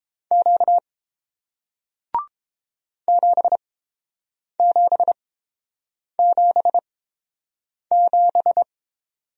Morse code, heard as a keyed tone, about 30 wpm, spaced as for 7 wpm, 700 Hz. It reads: Q 7777